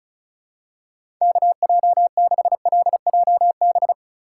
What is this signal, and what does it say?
Morse code, heard as a keyed tone, about 36 wpm, 700 Hz. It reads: KJ6LJB
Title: USA Call Signs - 6 Characters - 4x Character Spacing - 35wpm